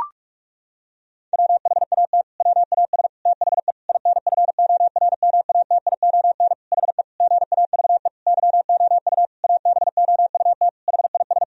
Morse code, heard as a keyed tone, about 43 wpm, 700 Hz. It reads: WHAT WAS THE INFORMATION HE GAVE YOU ABOUT HIS
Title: Sentences from Top 1000 Words - Review Part 3 - 45wpm